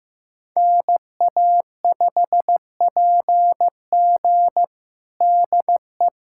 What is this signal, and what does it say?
Morse code, heard as a keyed tone, about 15 wpm, 700 Hz. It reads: NA5PG DE
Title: USA Callsign DE Callsign - 5 Characters - 15wpm